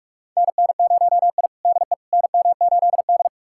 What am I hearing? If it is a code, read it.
Morse code, heard as a keyed tone, about 46 wpm, 700 Hz. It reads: NN0I DE NM8D
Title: USA Callsign DE Callsign - 4 Characters - 45wpm